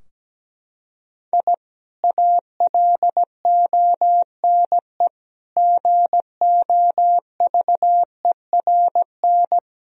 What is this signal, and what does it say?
Morse code, heard as a keyed tone, about 17 wpm, 700 Hz. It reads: I ALONE GOVERN
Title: Sentences from Top 1000 Words - Review Part 2 - 17wpm